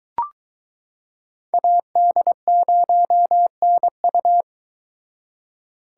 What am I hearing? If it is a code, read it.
Morse code, heard as a keyed tone, about 22 wpm, 700 Hz. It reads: AD0NU